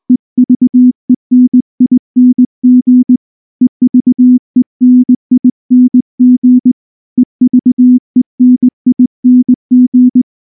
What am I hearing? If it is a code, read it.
Morse code, heard as a keyed tone, about 19 wpm, 250 Hz. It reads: EVENING EVENING EVENING
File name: morse / morse-training-20WPM-3